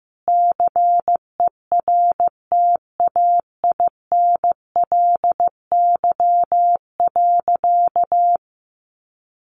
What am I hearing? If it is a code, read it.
Morse code, heard as a keyed tone, about 15 wpm, 700 Hz. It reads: CERTAINLY.
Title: Sentences from Top 900 Words 15wpm